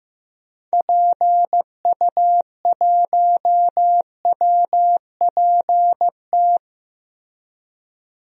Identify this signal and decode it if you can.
Morse code, heard as a keyed tone, about 15 wpm, 700 Hz. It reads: PU1WPT
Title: DXCC-100 Call Signs - 6 Characters - 15wpm